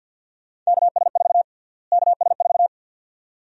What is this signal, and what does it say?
Morse code, heard as a keyed tone, about 51 wpm, 700 Hz. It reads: KS4 KS4
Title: USA Call Sign Prefixes - Repeated 2x - 50wpm